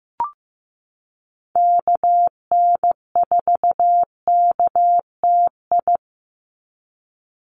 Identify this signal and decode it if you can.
Morse code, heard as a keyed tone, about 15 wpm, 700 Hz. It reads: KN4KTI
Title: USA Call Signs 15wpm